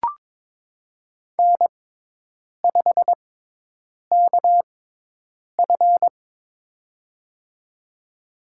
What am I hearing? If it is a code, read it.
Morse code, heard as a keyed tone, about 21 wpm, spaced as for 8 wpm, 700 Hz. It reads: N5KF
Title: USA Call Signs - 6x Character Spacing - 22wpm